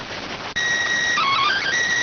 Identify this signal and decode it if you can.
SSTV (slow-scan television), PD 290.